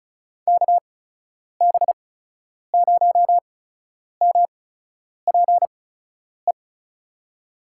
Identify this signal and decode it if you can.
Morse code, heard as a keyed tone, about 36 wpm, spaced as for 10 wpm, 700 Hz. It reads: KB0MPE
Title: USA Call Signs - 6 Characters - 8x Character Spacing - 35wpm